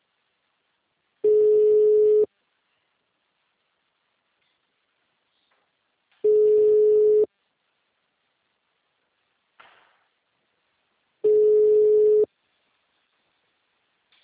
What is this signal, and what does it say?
Ringback tone (Europe, 425 Hz).